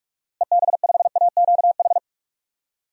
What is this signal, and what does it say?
Morse code, heard as a keyed tone, about 41 wpm, 700 Hz. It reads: EB5AQH
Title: DXCC-100 Call Signs - 6 Characters - Repeated 3x - 45wpm